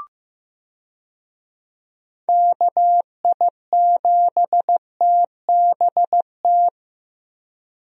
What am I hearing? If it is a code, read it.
Morse code, heard as a keyed tone, about 15 wpm, 700 Hz. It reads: KI7TBT
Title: USA Call Signs - 6 Characters - Repeated 3x - 15wpm